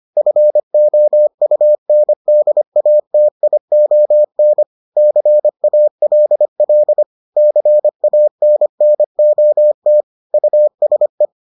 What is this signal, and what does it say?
Morse code, heard as a keyed tone, about 25 wpm, 600 Hz. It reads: FOUNDATION CALL CANNOT USE